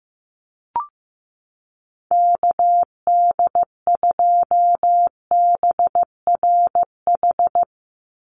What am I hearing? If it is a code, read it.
Morse code, heard as a keyed tone, about 15 wpm, 700 Hz. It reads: KD2BRH
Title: USA Call Signs 15wpm